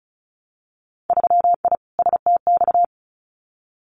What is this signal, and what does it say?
Morse code, heard as a keyed tone, about 36 wpm, 700 Hz. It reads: 3I STX
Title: Winter Field Day Exchanges 35wpm